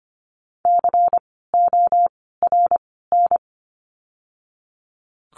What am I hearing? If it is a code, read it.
Morse code, heard as a keyed tone, about 25 wpm, spaced as for 16 wpm, 700 Hz. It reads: CORN
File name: course / 25:10wpm / 094 Lesson 16f - Letter c - Words 25wpm